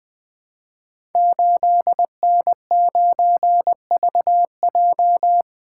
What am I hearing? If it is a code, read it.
Morse code, heard as a keyed tone, about 20 wpm, 700 Hz. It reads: 8N9VJ
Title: Code Groups - 5 Letter:Numbers - 20wpm